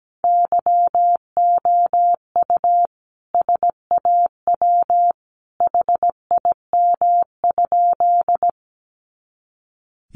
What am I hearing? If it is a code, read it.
Morse code, heard as a keyed tone, about 17 wpm, 700 Hz. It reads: YOU SAW HIM?
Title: Sentences from Top 300 Words 17wpm